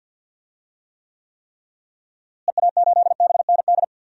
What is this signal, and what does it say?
Morse code, heard as a keyed tone, about 52 wpm, 700 Hz. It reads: EA8BND
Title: DXCC-100 Call Signs - 6 Characters - Repeated 3x - 50wpm